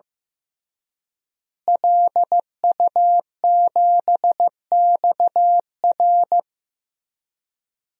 Morse code, heard as a keyed tone, about 15 wpm, 700 Hz. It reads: LU7XR
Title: DXCC-100 Call Signs - 5 Characters - Repeated 3x - 15wpm